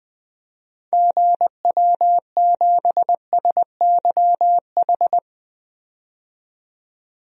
Morse code, heard as a keyed tone, about 20 wpm, 700 Hz. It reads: GW7SYH
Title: DXCC-100 Call Signs - 6 Characters - 20wpm